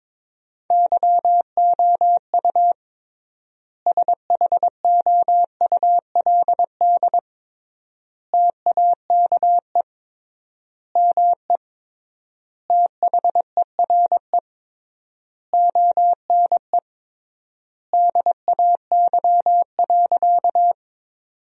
Morse code, heard as a keyed tone, about 22 wpm, 700 Hz. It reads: YOU SHOULD TAKE ME THERE ONE DAY.
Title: 3x Word-Spacing - Sentences from Top 300 Words - 22wpm